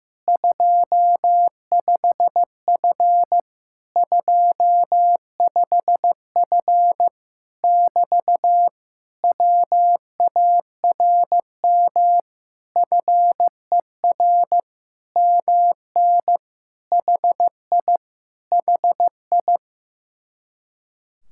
Morse code, heard as a keyed tone, about 15 wpm, 700 Hz. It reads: 25F 25F <BT> WARM FER MN HI HI
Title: Speed-Racing Common QSO Phrases Encore - 15wpm